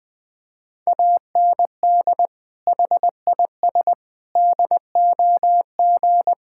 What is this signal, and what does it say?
Morse code, heard as a keyed tone, about 20 wpm, 700 Hz. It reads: AND HIS DOG
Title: Sentences with 2-Word Phrases - Spoken Beginning - 20wpm